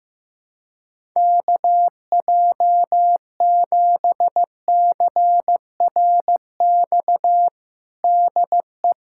Morse code, heard as a keyed tone, about 15 wpm, 700 Hz. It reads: KJ7CRX DE
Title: USA Callsign DE Callsign 15wpm